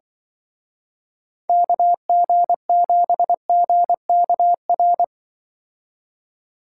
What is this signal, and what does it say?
Morse code, heard as a keyed tone, about 23 wpm, 700 Hz. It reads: KG7GKR
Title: Speed Racing - USA 2x3 Call Signs 20wpm